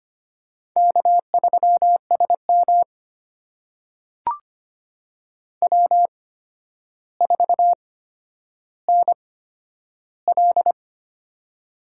Morse code, heard as a keyed tone, about 25 wpm, spaced as for 11 wpm, 700 Hz. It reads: K3SM W4NL
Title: USA Call Signs - 8x Character Spacing - 25wpm